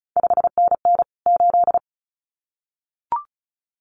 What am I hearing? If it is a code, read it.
Morse code, heard as a keyed tone, about 36 wpm, 700 Hz. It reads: 5NN 8
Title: CQ WW DX Contest Exchange - 35wpm